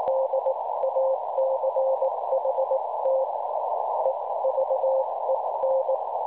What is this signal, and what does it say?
Morse code, heard as a keyed tone, about 18 wpm, spaced as for 16 wpm, 530 Hz, with noise. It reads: DACHT EVEN
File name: cwronde / websdr recording .6kHz